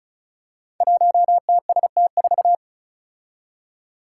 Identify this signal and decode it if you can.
Morse code, heard as a keyed tone, about 36 wpm, 700 Hz. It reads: 1TST4
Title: Five Challenging Letter-Number Combinations 35wpm